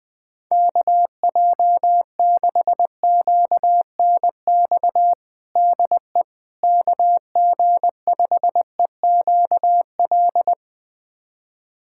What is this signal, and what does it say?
Morse code, heard as a keyed tone, about 20 wpm, 700 Hz. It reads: KJ6QNX DE KG5EQL
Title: USA Callsign DE Callsign - 6 Characters - 20wpm